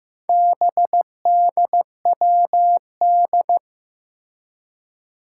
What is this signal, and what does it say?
Morse code, heard as a keyed tone, about 15 wpm, 700 Hz. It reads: BDWD
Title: Four Challenging Letter-Number Combinations 15wpm